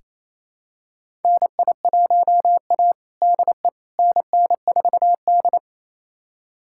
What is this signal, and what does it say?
Morse code, heard as a keyed tone, about 28 wpm, 700 Hz. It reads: NI1A DE NN4D
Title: USA Callsign DE Callsign - 4 Characters - 28wpm